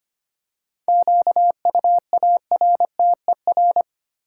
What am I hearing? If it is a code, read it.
Morse code, heard as a keyed tone, about 25 wpm, 700 Hz. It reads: QUARTER
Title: Seven Letter Words - 25wpm